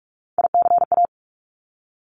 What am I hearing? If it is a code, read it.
Morse code, heard as a keyed tone, about 46 wpm, 700 Hz. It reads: ICA
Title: Top Three Letter Combinations in Words 45wpm